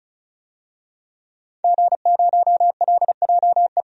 Morse code, heard as a keyed tone, about 34 wpm, 700 Hz. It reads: G0LJE
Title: DXCC-100 Call Signs - 5 Characters - Repeated 2x - 35wpm